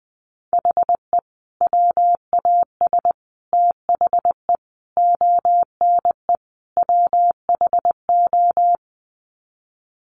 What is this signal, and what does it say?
Morse code, heard as a keyed tone, about 20 wpm, 700 Hz. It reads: HE WAS THE ONE WHO
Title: Sets of 5 Words Encore2 20wpm